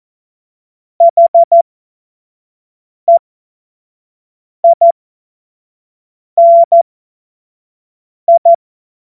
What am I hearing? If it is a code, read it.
Morse code, heard as a keyed tone, about 14 wpm, spaced as for 5 wpm, 680 Hz. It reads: HEINI
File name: Morsekurs / Witze / Witz 15. November 2022 25 F70